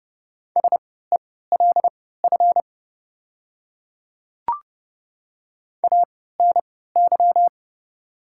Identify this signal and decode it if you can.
Morse code, heard as a keyed tone, about 28 wpm, spaced as for 17 wpm, 690 Hz. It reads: SELF ANY
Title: 3x Character Spacing - Top 200 Words - 30wpm